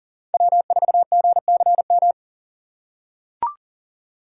Morse code, heard as a keyed tone, about 37 wpm, 700 Hz. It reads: W4GCM
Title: USA Call Signs - 5 Characters - 3x Character Spacing - 40wpm